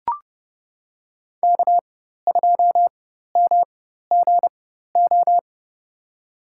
Morse code, heard as a keyed tone, about 32 wpm, spaced as for 14 wpm, 700 Hz. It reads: K2MGO
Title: USA Call Signs - 5 Characters - 4x Character Spacing - 30wpm